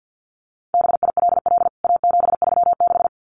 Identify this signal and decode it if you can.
Morse code, heard as a keyed tone, about 48 wpm, 700 Hz. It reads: BILL A736
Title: CWT Exchange 50wpm